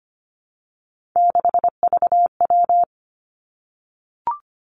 Morse code, heard as a keyed tone, about 25 wpm, 700 Hz. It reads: 6VW
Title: Three Challenging Letter-Number Combinations 25wpm